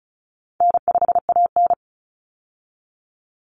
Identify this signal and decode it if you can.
Morse code, heard as a keyed tone, about 34 wpm, 700 Hz. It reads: N5AN
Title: Call Signs - 35wpm